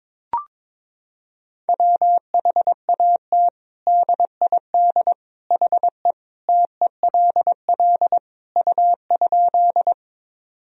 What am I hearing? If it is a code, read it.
Morse code, heard as a keyed tone, about 22 wpm, 700 Hz. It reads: WHAT DID HE TELL U?
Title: Sentences from Top 200 Words and Abbreviations 22wpm